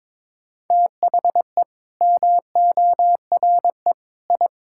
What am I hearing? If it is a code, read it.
Morse code, heard as a keyed tone, about 22 wpm, 700 Hz. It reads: THE MORE I
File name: practice / 22wpm / Sets of 8 Words - 22wpm